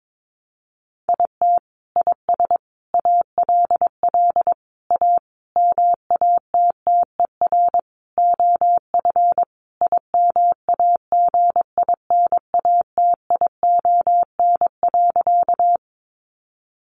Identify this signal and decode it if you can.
Morse code, heard as a keyed tone, about 22 wpm, 700 Hz. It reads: IT IS ALL A MATTER OF IMAGINATION.
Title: Sentences from Top 1300 Words 22wpm